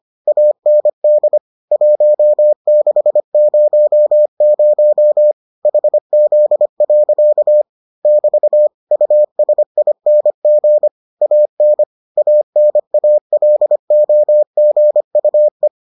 Morse code, heard as a keyed tone, about 25 wpm, 600 Hz. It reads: AND 1600 HZ. <BT> USING AN ANALOGUE